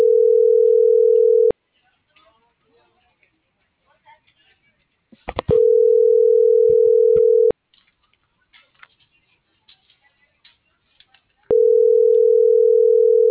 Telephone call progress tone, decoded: ringback tone

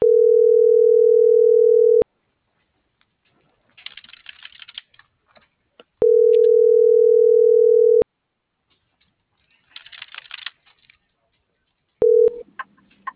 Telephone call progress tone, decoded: ringback tone

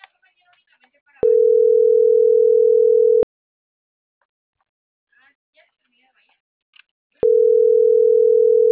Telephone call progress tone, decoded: ringback tone

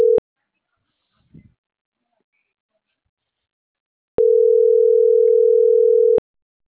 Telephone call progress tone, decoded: ringback tone

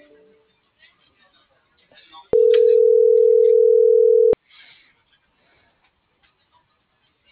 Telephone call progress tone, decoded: ringback tone

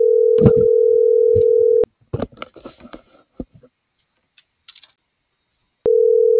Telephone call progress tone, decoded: ringback tone

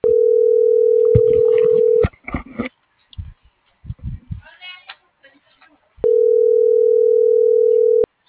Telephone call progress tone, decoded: ringback tone

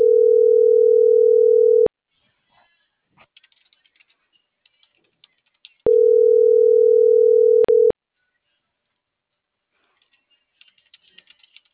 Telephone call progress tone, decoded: ringback tone

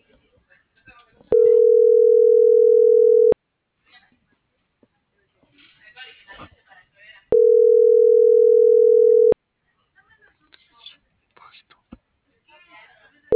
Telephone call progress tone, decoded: ringback tone

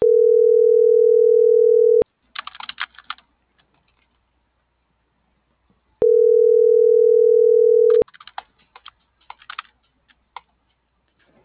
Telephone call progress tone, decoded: ringback tone

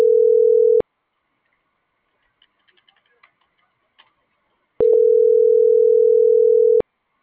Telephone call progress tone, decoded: ringback tone

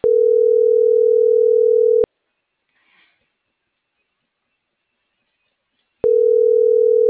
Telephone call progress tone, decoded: ringback tone

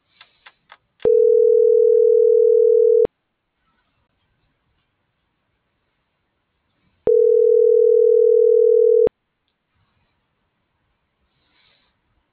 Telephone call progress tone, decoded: ringback tone

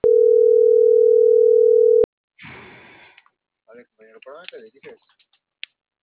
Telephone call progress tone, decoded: ringback tone

